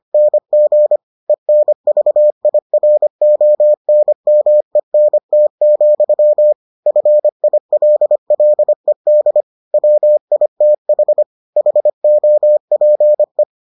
Morse code, heard as a keyed tone, about 25 wpm, 600 Hz. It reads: NG ENVIRONMENT, FILLED WITH HOPE